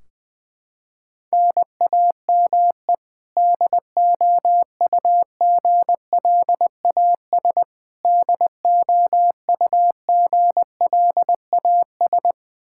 Morse code, heard as a keyed tone, about 21 wpm, 700 Hz. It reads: NAME DOUGLAS DOUGLAS
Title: Practice QSO - Exchange 1 - Beginning - 20wpm